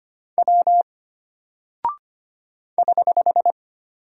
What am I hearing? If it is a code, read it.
Morse code, heard as a keyed tone, about 25 wpm, 700 Hz. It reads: W <HH>
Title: Single Character (Letter:Number:Punctuation:Prosign) Alt 25wpm